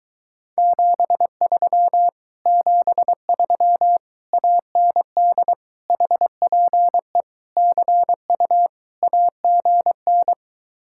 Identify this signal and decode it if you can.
Morse code, heard as a keyed tone, about 23 wpm, 700 Hz. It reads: 73 73 AND HPE CU AGN